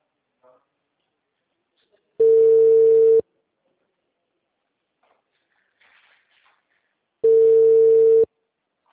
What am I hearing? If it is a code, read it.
Ringback tone (China, 450 Hz).